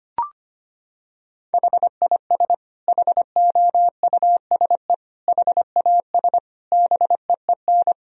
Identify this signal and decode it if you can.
Morse code, heard as a keyed tone, about 25 wpm, 700 Hz. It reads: HIS HOUSE HAS BEEN
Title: Sentences from Top 1000 Words - Review Part 3 - 25wpm